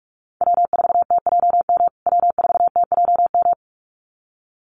Morse code, heard as a keyed tone, about 44 wpm, 700 Hz. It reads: W4TJM W4TJM
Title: USA Call Signs Repeated 2x 45wpm